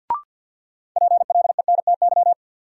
Morse code, heard as a keyed tone, about 49 wpm, 700 Hz. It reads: PLENTY
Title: Sentences from Top 1300 Words - New Word Drill - 50wpm